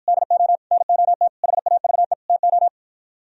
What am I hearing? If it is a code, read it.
Morse code, heard as a keyed tone, about 52 wpm, 700 Hz. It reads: DO NOT HAVE TO